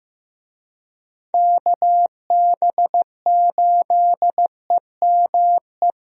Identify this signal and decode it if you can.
Morse code, heard as a keyed tone, about 14 wpm, 700 Hz. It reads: KB8EME
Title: USA Call Signs - 6 Characters - Repeated 3x - 15wpm